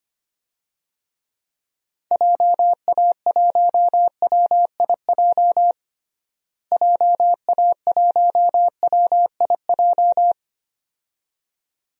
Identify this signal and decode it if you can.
Morse code, heard as a keyed tone, about 25 wpm, 700 Hz. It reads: JA1WIJ JA1WIJ
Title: DXCC-100 Call Signs - 6 Characters - Repeated 2x - 25wpm